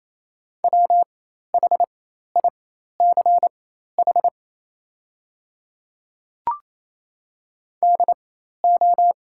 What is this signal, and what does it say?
Morse code, heard as a keyed tone, about 28 wpm, spaced as for 13 wpm, 700 Hz. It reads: WHICH DO
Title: 4x Character Spacing - Top 100 Words - 28wpm